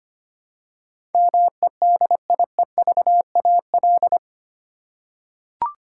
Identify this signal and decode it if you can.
Morse code, heard as a keyed tone, about 25 wpm, 700 Hz. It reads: MEDIEVAL
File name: 8 Letter Words - First 3 Letters Given - 25wpm